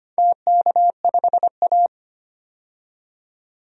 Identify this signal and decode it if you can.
Morse code, heard as a keyed tone, about 25 wpm, 690 Hz. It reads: TK5A
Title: DXCC-100 Call Signs - 4 Characters - 25wpm